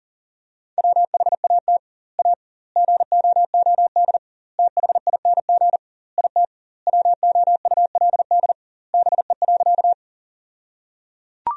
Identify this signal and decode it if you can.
Morse code, heard as a keyed tone, about 37 wpm, 700 Hz. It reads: WHAT A GOOD THING IT WOULD BE.
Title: 2x Word-Spacing - Sentences from Top 200 Words - 40wpm